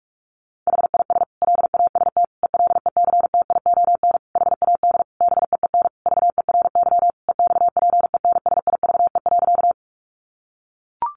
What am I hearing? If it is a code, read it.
Morse code, heard as a keyed tone, about 44 wpm, 700 Hz. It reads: HIS LAST ELECTION HAD BEEN VERY EXPENSIVE.
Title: Sentences from Top 1500 Words 45wpm